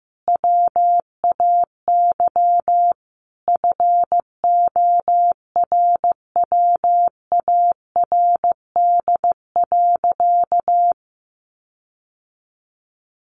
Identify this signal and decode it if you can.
Morse code, heard as a keyed tone, about 15 wpm, 700 Hz. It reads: WAY FORWARD.
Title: Sentences from Top 1500 Words 15wpm